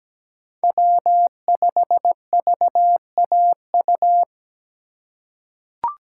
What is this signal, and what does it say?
Morse code, heard as a keyed tone, about 17 wpm, 700 Hz. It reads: W5VAU
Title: USA Call Signs - 5 Characters - 4x Character Spacing - 17wpm